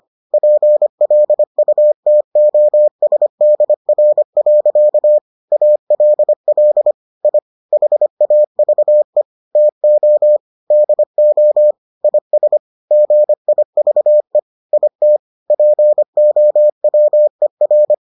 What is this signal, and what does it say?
Morse code, heard as a keyed tone, about 25 wpm, 600 Hz. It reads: PLUTOSDR. ALL I HAVE TO DO IS GIVE IT POWER